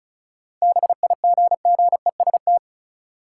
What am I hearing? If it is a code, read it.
Morse code, heard as a keyed tone, about 36 wpm, 700 Hz. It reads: BIGGEST
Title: Sets of 4 Words - New Word Drill - 35wpm